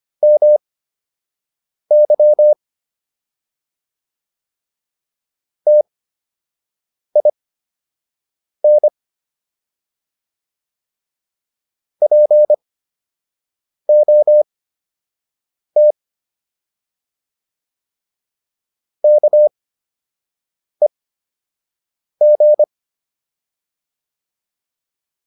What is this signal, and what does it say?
Morse code, heard as a keyed tone, about 25 wpm, spaced as for 6 wpm, 600 Hz. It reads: MY TIN POT KEG